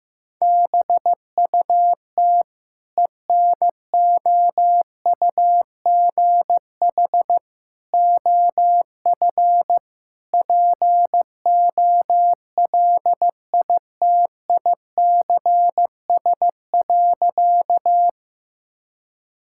Morse code, heard as a keyed tone, about 16 wpm, 700 Hz. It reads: BUT ENOUGH OF POLITICS.
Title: Sentences from Top 1000 Words - Review Part 6 - 15wpm